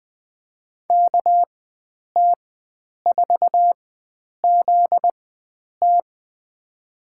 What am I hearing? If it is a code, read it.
Morse code, heard as a keyed tone, about 20 wpm, spaced as for 9 wpm, 700 Hz. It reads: KT4ZT
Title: USA Call Signs - 5 Characters - 4x Character Spacing - 20wpm